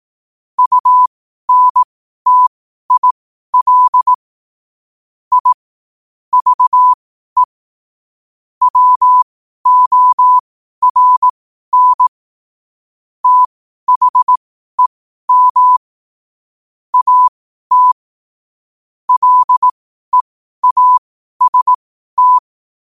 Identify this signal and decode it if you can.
Morse code, heard as a keyed tone, about 18 wpm, spaced as for 12 wpm, 1000 Hz. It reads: UNTIL I VE WORN THEM AT LEAST
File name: qotdfiles / 12wpm